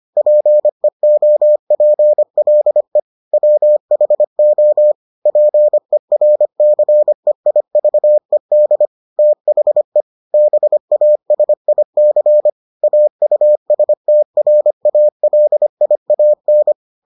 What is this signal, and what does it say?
Morse code, heard as a keyed tone, about 25 wpm, 600 Hz. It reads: PEOPLE WHO PERCEIVED THE BASIC AUSTRALIAN